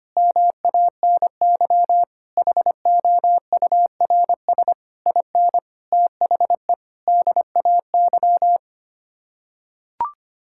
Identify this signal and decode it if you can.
Morse code, heard as a keyed tone, about 25 wpm, 700 Hz. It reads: MANY HOURS IN THE DAY
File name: Sets of 9 Words - 25wpm